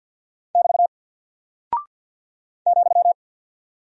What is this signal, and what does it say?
Morse code, heard as a keyed tone, about 48 wpm, 700 Hz. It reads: <BT> ,
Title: Single Character (Letter:Number:Punctuation:Prosign) Alt 50wpm